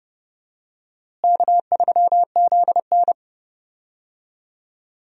Morse code, heard as a keyed tone, about 30 wpm, 700 Hz. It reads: K3ZN
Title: USA Call Signs - 4 Characters - Repeated 2x - 30wpm